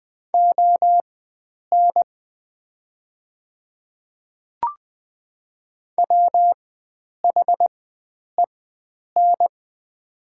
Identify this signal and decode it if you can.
Morse code, heard as a keyed tone, about 21 wpm, spaced as for 9 wpm, 700 Hz. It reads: ON WHEN
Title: 4x Character Spacing - Top 100 Words - 20wpm